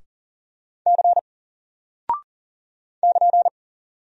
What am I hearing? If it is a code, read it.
Morse code, heard as a keyed tone, about 40 wpm, 700 Hz. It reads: C <KN>
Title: Single Character (Letter:Number:Punctuation:Prosign) Alt 40wpm